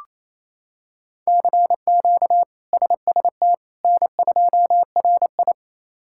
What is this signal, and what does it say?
Morse code, heard as a keyed tone, about 28 wpm, 700 Hz. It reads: CQ SST N2RI
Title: K1USN SST QSO Practice - 28wpm